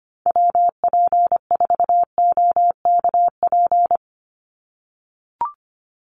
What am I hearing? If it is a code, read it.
Morse code, heard as a keyed tone, about 25 wpm, 700 Hz. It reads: WP4OKP
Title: USA Call Signs 25wpm